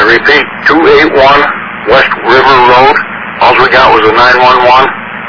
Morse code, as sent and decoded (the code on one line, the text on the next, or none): none